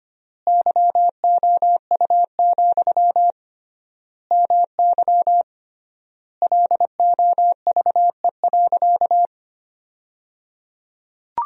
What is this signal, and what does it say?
Morse code, heard as a keyed tone, about 25 wpm, 700 Hz. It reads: YOU, MY LOVE.
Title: 3x Word-Spacing - Sentences from Top 400 Words - 25wpm